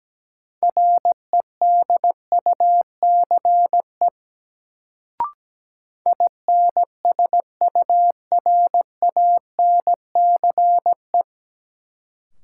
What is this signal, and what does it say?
Morse code, heard as a keyed tone, about 17 wpm, 700 Hz. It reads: REDUCE INSURANCE
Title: Sentences from Top 1100 Words - New Word Drill Alt - 17wpm